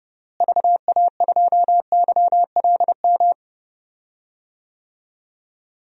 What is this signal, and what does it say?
Morse code, heard as a keyed tone, about 31 wpm, 700 Hz. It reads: VA2YLM
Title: DXCC-100 Call Signs - 6 Characters - Repeated 2x - 30wpm